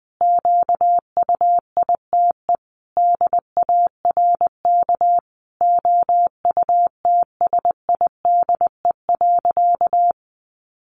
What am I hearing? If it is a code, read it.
Morse code, heard as a keyed tone, about 20 wpm, 700 Hz. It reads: QUITE DARK OUTSIDE.
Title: Sentences from Top 900 Words 20wpm